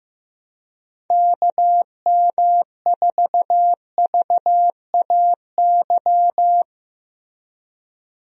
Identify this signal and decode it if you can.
Morse code, heard as a keyed tone, about 16 wpm, 700 Hz. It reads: KM4VAY
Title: USA Call Signs - 6 Characters - Repeated 2x - 15wpm